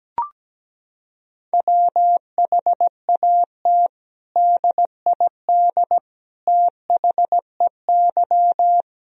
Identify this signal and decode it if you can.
Morse code, heard as a keyed tone, about 17 wpm, 700 Hz. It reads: WHAT DID THEY
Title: Sentences from Top 1000 Words - Review Part 4 - 17wpm